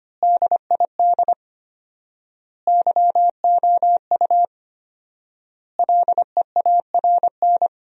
Morse code, heard as a keyed tone, about 25 wpm, 700 Hz. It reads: DID YOU LEARN
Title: 4x Word-Spacing - Sentences from Top 300 Words - 25wpm